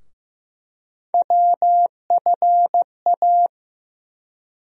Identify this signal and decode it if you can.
Morse code, heard as a keyed tone, about 15 wpm, 700 Hz. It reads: WFA
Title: USA Call Sign Postfixes - 3 Letters 15wpm